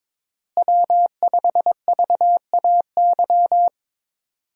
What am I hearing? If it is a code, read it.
Morse code, heard as a keyed tone, about 22 wpm, 700 Hz. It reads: W5VAY
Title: USA Call Signs - 5 Characters - Repeated 2x - 22wpm